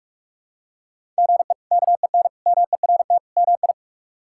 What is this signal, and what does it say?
Morse code, heard as a keyed tone, about 42 wpm, 700 Hz. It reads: GE KEN MERT MI
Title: K1USN SST QSO Practice - 45wpm